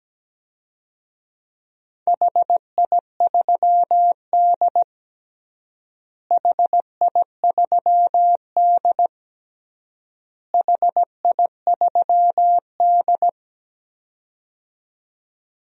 Morse code, heard as a keyed tone, about 17 wpm, 700 Hz. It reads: HI3D HI3D HI3D
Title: DXCC-100 Call Signs - 4 Characters - Repeated 3x - 17wpm